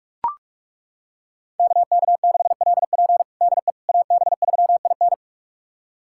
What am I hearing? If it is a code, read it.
Morse code, heard as a keyed tone, about 41 wpm, 700 Hz. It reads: KK6LP DE AB3IN
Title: USA Callsign DE Callsign - 5 Characters - 45wpm